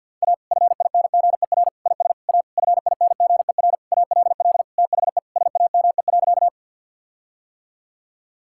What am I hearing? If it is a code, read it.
Morse code, heard as a keyed tone, about 50 wpm, 700 Hz. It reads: A FINGER IS A FINGER ALL THE SAME.
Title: Sentences from Top 1600 Words 50wpm